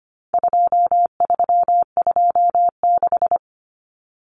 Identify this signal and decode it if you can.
Morse code, heard as a keyed tone, about 25 wpm, 700 Hz. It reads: 2326
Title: Common Cut Numbers - 4-Digit 25wpm